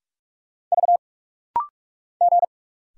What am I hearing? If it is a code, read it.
Morse code, heard as a keyed tone, about 47 wpm, 700 Hz. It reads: V G
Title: Single Letter-Number - Rapid-Fire 45wpm